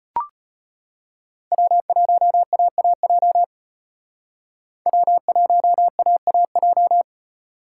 Morse code, heard as a keyed tone, about 32 wpm, 700 Hz. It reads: W1AAJ W1AAJ